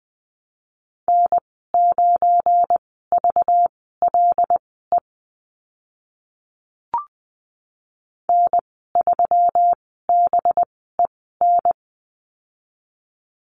Morse code, heard as a keyed tone, about 20 wpm, spaced as for 14 wpm, 700 Hz. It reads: N9VLE N3BEN
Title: 193 Lesson 31g - Letter v - Call Signs 20wpm